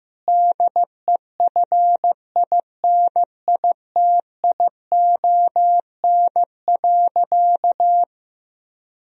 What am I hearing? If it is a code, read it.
Morse code, heard as a keyed tone, about 15 wpm, 700 Hz. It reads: DEFINITION.
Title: Sentences from Top 1000 Words - Review Part 4 - 15wpm